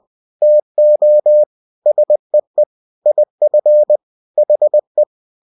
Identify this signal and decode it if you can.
Morse code, heard as a keyed tone, about 20 wpm, 600 Hz. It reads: TO SEE IF HE
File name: cw-books / winnie-the-pooh / winnie-20wpm-page-0049